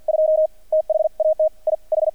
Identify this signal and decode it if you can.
Morse code, heard as a keyed tone, about 48 wpm, 630 Hz. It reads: 2 THAT I H
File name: archived / CRSnet